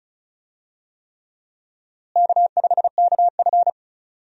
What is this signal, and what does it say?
Morse code, heard as a keyed tone, about 36 wpm, 700 Hz. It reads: K5KF
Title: USA Call Signs - 4 Characters - Repeated 3x - 35wpm